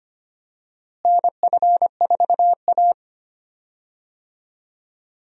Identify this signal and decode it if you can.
Morse code, heard as a keyed tone, about 25 wpm, 700 Hz. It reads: NF4A
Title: USA Call Signs - 4 Characters - Repeated 2x - 25wpm